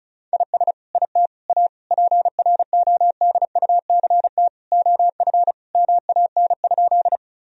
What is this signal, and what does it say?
Morse code, heard as a keyed tone, about 36 wpm, 700 Hz. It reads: IS IT A PRODUCT OF MAN?
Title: Sentences from Top 1000 Words - Review Part 2 - 35wpm